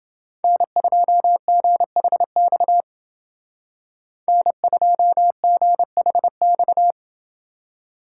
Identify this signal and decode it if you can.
Morse code, heard as a keyed tone, about 28 wpm, 700 Hz. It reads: N2GHX N2GHX